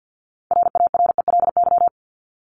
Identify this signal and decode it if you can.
Morse code, heard as a keyed tone, about 50 wpm, 700 Hz. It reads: RARELY
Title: Top 1251-1500 Words 50wpm